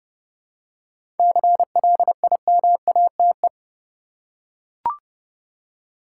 Morse code, heard as a keyed tone, about 30 wpm, 700 Hz. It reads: CLIMATE
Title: Seven Letter Words - 30wpm